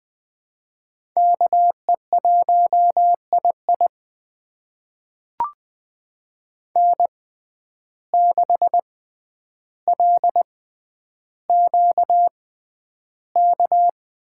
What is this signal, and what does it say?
Morse code, heard as a keyed tone, about 20 wpm, spaced as for 10 wpm, 700 Hz. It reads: KE1II N6LQK
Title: USA Call Signs - 6x Character Spacing - 20wpm